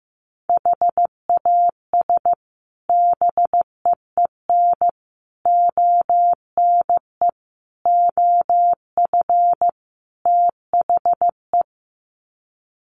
Morse code, heard as a keyed tone, about 14 wpm, 700 Hz. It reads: HAS BEEN ONE OF THE
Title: Sets of 5 Words Encore 15wpm